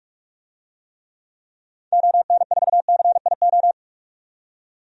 Morse code, heard as a keyed tone, about 46 wpm, 700 Hz. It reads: ON4CIO
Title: DXCC-100 Call Signs - 6 Characters - 45wpm